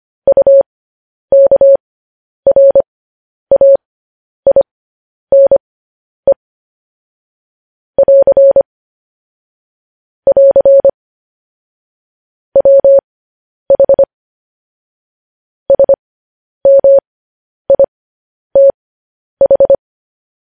Morse code, heard as a keyed tone, about 25 wpm, spaced as for 10 wpm, 550 Hz. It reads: UKRAINE <AR> <AR> WH SMITH
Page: news-headlines-in-morse-code-at-10-wpm